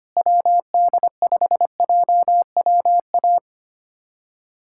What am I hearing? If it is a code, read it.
Morse code, heard as a keyed tone, about 25 wpm, 700 Hz. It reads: WD5JWA
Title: USA Callsign DE Callsign - 6 Characters - 25wpm